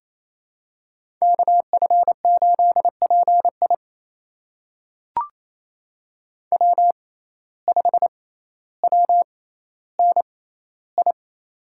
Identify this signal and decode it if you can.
Morse code, heard as a keyed tone, about 28 wpm, spaced as for 14 wpm, 700 Hz. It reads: KF8PI W5WNI
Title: USA Call Signs - 5 Characters - 6x Character Spacing - 28wpm